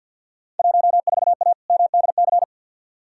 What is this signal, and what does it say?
Morse code, heard as a keyed tone, about 53 wpm, 700 Hz. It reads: 14A MDC
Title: Field Day Exchanges Encore - 50wpm